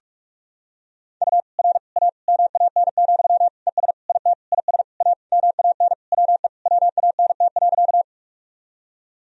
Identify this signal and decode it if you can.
Morse code, heard as a keyed tone, about 41 wpm, 700 Hz. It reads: U R A MAN, ES IT IS A MAN WE WANT.
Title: Sentences from Top 200 Words and Abbreviations 45wpm